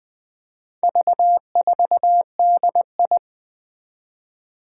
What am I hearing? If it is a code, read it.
Morse code, heard as a keyed tone, about 20 wpm, 700 Hz. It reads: V4DI
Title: Four Challenging Letter-Number Combinations 20wpm